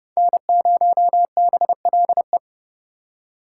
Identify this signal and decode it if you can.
Morse code, heard as a keyed tone, about 40 wpm, 700 Hz. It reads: N0BLE